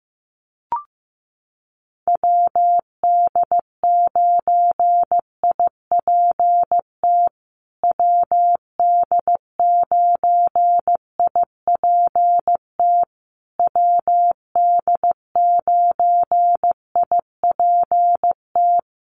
Morse code, heard as a keyed tone, about 15 wpm, 700 Hz. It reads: WD9IPT WD9IPT WD9IPT
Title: USA Call Signs Repeated 3x 15wpm